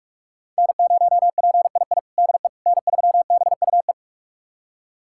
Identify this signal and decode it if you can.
Morse code, heard as a keyed tone, about 46 wpm, 700 Hz. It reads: N0PII DE N3BUE
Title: USA Callsign DE Callsign - 5 Characters - 45wpm